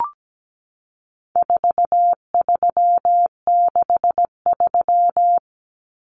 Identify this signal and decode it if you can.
Morse code, heard as a keyed tone, about 17 wpm, 700 Hz. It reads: 4363
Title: Common Cut Numbers - 4-Digit 17wpm